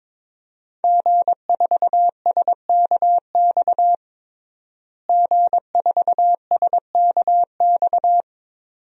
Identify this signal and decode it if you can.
Morse code, heard as a keyed tone, about 22 wpm, 700 Hz. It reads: G4SKX G4SKX
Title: DXCC-100 Call Signs - 5 Characters - Repeated 2x - 22wpm